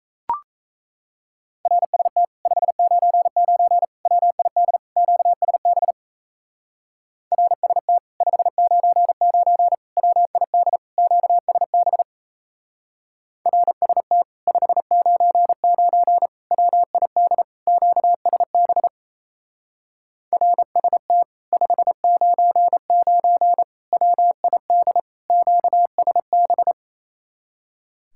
Morse code, heard as a keyed tone, about 35 wpm, 700 Hz. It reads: RST 599 WID QSB RST 599 WID QSB RST 599 WID QSB RST 599 WID QSB
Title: Speed-Racing Common QSO Phrases Encore - 28wpm